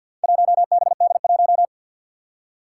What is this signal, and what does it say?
Morse code, heard as a keyed tone, about 50 wpm, 700 Hz. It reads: 1BD1
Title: Four Challenging Letter-Number Combinations 50wpm